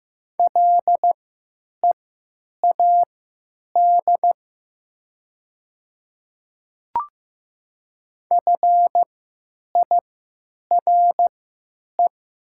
Morse code, heard as a keyed tone, about 15 wpm, spaced as for 9 wpm, 700 Hz. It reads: LEAD FIRE
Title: 3x Character Spacing - Top 500 Words - 15wpm